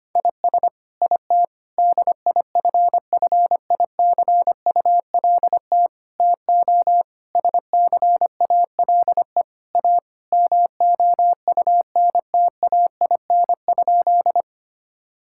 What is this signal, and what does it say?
Morse code, heard as a keyed tone, about 25 wpm, 700 Hz. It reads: IS IT DIFFICULT TO SCALE A MOUNTAIN?
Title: Sentences from Top 1000 Words - Review Part 5 - 25wpm